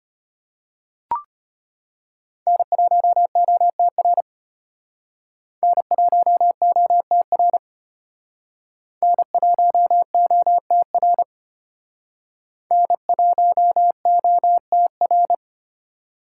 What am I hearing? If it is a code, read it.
Morse code, heard as a keyed tone, about 33 wpm, 700 Hz. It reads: N1OTR N1OTR N1OTR N1OTR